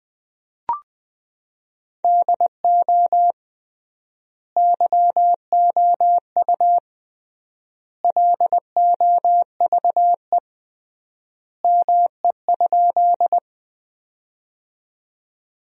Morse code, heard as a keyed tone, about 20 wpm, 700 Hz. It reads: DO YOU LOVE ME?
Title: 3x Word-Spacing - Sentences from Top 400 Words - 20wpm